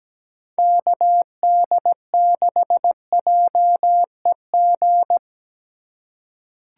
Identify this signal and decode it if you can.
Morse code, heard as a keyed tone, about 17 wpm, 700 Hz. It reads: KD6JEG
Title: USA Call Signs - 6 Characters - 8x Character Spacing - 17wpm